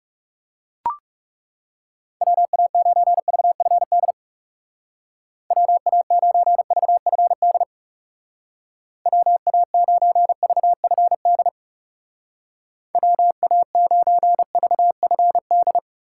Morse code, heard as a keyed tone, about 36 wpm, 700 Hz. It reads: WA9VFD WA9VFD WA9VFD WA9VFD
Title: Speed Racing - USA 2x3 Call Signs 30wpm